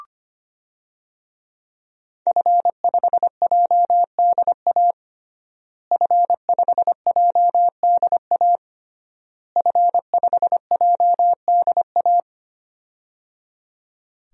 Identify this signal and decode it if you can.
Morse code, heard as a keyed tone, about 25 wpm, 700 Hz. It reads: F5JDA F5JDA F5JDA
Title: DXCC-100 Call Signs - 5 Characters - Repeated 3x - 25wpm